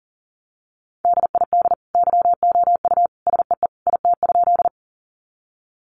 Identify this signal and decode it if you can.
Morse code, heard as a keyed tone, about 40 wpm, 700 Hz. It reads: DID YOU SEE IT?
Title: Sentences from Top 100 Words 40wpm